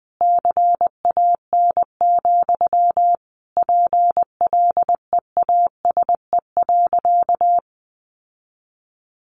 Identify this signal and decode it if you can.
Morse code, heard as a keyed tone, about 20 wpm, 700 Hz. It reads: CAN, PLEASE.
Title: Sentences from Top 1500 Words 20wpm